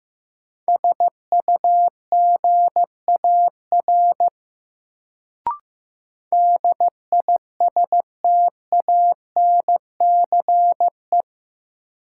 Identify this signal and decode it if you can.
Morse code, heard as a keyed tone, about 15 wpm, 700 Hz. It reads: SUGAR DISTANCE
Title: Sentences from Top 1200 Words - New Word Drill Alt - 15wpm